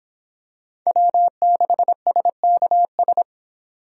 Morse code, heard as a keyed tone, about 25 wpm, 700 Hz. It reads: W6SKS